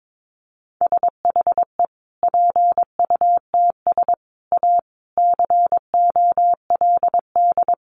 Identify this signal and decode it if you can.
Morse code, heard as a keyed tone, about 22 wpm, 700 Hz. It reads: SHE PUTS A COLD
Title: Easy Phrases 22wpm